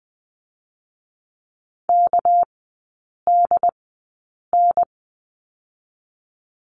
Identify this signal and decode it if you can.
Morse code, heard as a keyed tone, about 20 wpm, 700 Hz. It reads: K D N
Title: 198 Lesson 32e - Letter k - Three Characters 20wpm